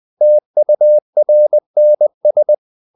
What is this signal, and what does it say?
Morse code, heard as a keyed tone, about 20 wpm, 600 Hz. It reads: TURNS
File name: the old man and the sea part 2 20wpm